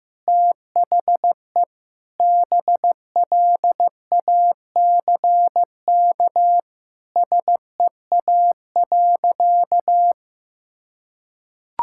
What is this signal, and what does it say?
Morse code, heard as a keyed tone, about 15 wpm, 700 Hz. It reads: THE BLACK SEA.
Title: Sentences from Top 1000 Words - Review Part 5 - 15wpm